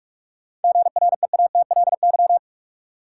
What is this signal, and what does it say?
Morse code, heard as a keyed tone, about 45 wpm, 700 Hz. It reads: GREATLY
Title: Sentences from Top 1400 Words - New Word Drill Alt - 45wpm